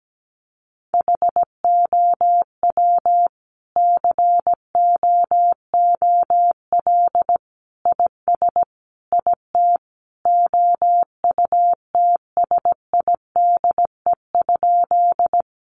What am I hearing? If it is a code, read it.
Morse code, heard as a keyed tone, about 17 wpm, 700 Hz. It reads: HOW COOL IS IT OUTSIDE?
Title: Sentences from Top 1200 Words 17wpm